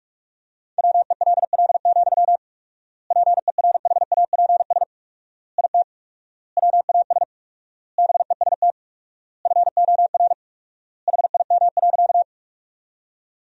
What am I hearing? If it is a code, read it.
Morse code, heard as a keyed tone, about 46 wpm, 690 Hz. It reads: WELL, PERHAPS IT WAS BEST FOR HIM.
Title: 4x Word-Spacing - Sentences from Top 500 Words - 45wpm